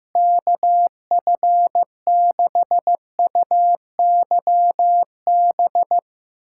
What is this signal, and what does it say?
Morse code, heard as a keyed tone, about 15 wpm, 700 Hz. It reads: KF6UYB
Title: USA Callsign DE Callsign - 6 Characters - 15wpm